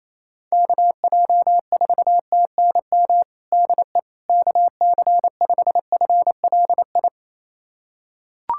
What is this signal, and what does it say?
Morse code, heard as a keyed tone, about 28 wpm, 700 Hz. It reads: KJ4TNM DE KC5FLI
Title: USA Callsign DE Callsign - 6 Characters - 28wpm